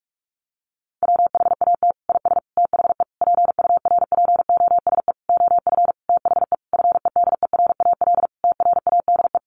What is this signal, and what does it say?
Morse code, heard as a keyed tone, about 46 wpm, 700 Hz. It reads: WHAT IS THE PURPOSE OF THE FEDERAL TRADE
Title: Sentences from Top 1400 Words 45wpm